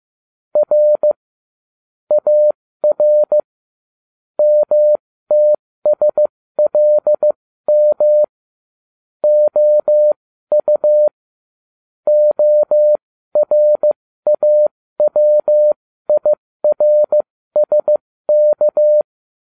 Morse code, heard as a keyed tone, about 15 wpm, spaced as for 13 wpm, 600 Hz. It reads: R AR MTSLM OU ORAWIRSK
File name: cw / ubacw12 2